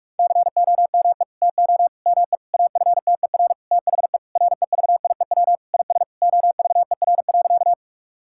Morse code, heard as a keyed tone, about 46 wpm, 700 Hz. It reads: COME TO ME AFTER THE REVIEW IS OVER.
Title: Sentences from Top 1000 Words - Review Part 6 - 45wpm